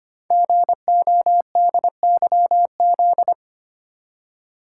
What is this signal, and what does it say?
Morse code, heard as a keyed tone, about 25 wpm, 700 Hz. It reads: GODYZ
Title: Code Groups - 5 Letters - 25wpm